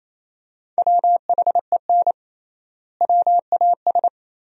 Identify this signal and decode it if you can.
Morse code, heard as a keyed tone, about 28 wpm, 700 Hz. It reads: WHEN WAS
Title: 3x Word-Spacing - Sentences from Top 100 Words - 28wpm